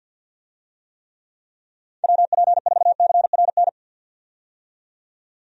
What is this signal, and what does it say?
Morse code, heard as a keyed tone, about 47 wpm, 700 Hz. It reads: WP4CRN
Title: USA Call Signs - 6 Characters - 50wpm